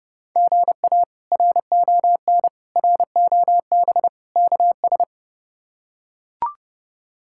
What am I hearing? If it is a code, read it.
Morse code, heard as a keyed tone, about 32 wpm, 700 Hz. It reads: GA RON ROB KS
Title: K1USN SST QSO Practice - 30wpm